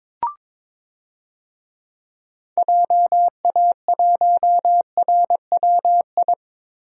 Morse code, heard as a keyed tone, about 21 wpm, 700 Hz. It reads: JA1RWI
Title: DXCC-100 Call Signs - 6 Characters - 22wpm